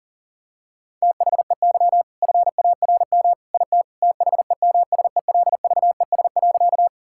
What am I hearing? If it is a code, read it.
Morse code, heard as a keyed tone, about 40 wpm, 700 Hz. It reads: THEY FARM IT THEMSELVES.
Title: Sentences from Top 1000 Words - Review Part 5 - 40wpm